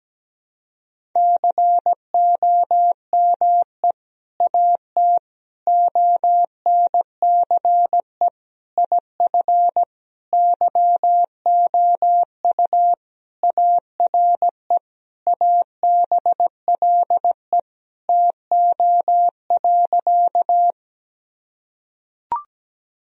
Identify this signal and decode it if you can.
Morse code, heard as a keyed tone, about 17 wpm, 700 Hz. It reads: COME AT ONCE IF YOU ARE ABLE TO.
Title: Sentences from Top 1000 Words - Review Part 2 - 17wpm